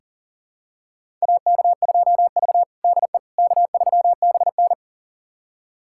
Morse code, heard as a keyed tone, about 37 wpm, 700 Hz. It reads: AK2V DE K3BN
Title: USA Callsign DE Callsign - 4 Characters - 40wpm